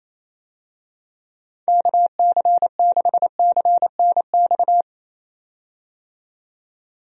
Morse code, heard as a keyed tone, about 27 wpm, 700 Hz. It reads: KC6CNX
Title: USA Call Signs - 6 Characters - 28wpm